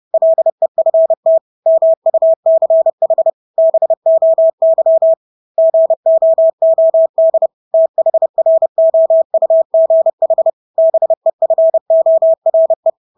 Morse code, heard as a keyed tone, about 30 wpm, 650 Hz. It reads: LEFT MUCH BOY GOOD THROUGH BEFORE
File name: Engelske ord 30-30 wpm